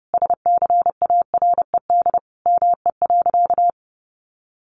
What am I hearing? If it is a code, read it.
Morse code, heard as a keyed tone, about 30 wpm, 700 Hz. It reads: SCARED ME.
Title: Sentences from Top 700 Words 30wpm